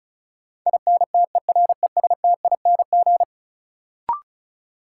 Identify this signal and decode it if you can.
Morse code, heard as a keyed tone, about 36 wpm, 700 Hz. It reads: INTERESTING
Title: Sentences from Top 1000 Words - New Word Drill - 35wpm